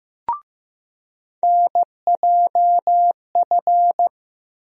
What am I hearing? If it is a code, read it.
Morse code, heard as a keyed tone, about 15 wpm, 700 Hz. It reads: NJF
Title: USA Call Sign Postfixes - 3 Letters 15wpm